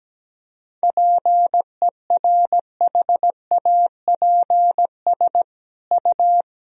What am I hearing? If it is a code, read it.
Morse code, heard as a keyed tone, about 17 wpm, 700 Hz. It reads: PERHAPS U
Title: Sentences from Top 500 Words and Abbreviations 17wpm